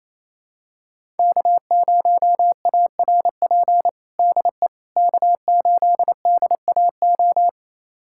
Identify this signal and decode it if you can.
Morse code, heard as a keyed tone, about 28 wpm, 700 Hz. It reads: K0ARP DE K8DAO
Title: USA Callsign DE Callsign - 5 Characters - 28wpm